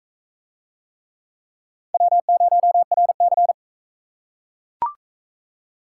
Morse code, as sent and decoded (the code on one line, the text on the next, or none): .-- ----- .-. -.-.
W0RC